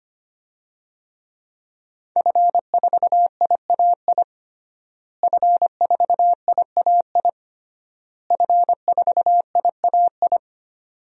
Morse code, heard as a keyed tone, about 25 wpm, 700 Hz. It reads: F4IAI F4IAI F4IAI
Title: DXCC-100 Call Signs - 5 Characters - Repeated 3x - 25wpm